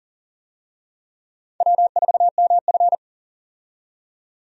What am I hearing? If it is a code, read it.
Morse code, heard as a keyed tone, about 40 wpm, 700 Hz. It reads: W4MF
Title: USA Call Signs - 4 Characters - 40wpm